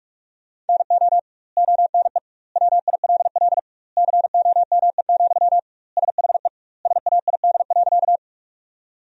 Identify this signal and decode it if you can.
Morse code, heard as a keyed tone, about 42 wpm, 700 Hz. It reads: NO ONE WILL COME, SHE SAID.
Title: 2x Word-Spacing - Sentences from Top 100 Words - 45wpm